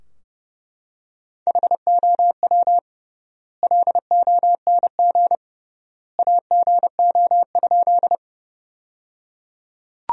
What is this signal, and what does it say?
Morse code, heard as a keyed tone, about 30 wpm, 700 Hz. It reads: HOW LONG AGO?
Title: 3x Word-Spacing - Sentences from Top 500 Words - 30wpm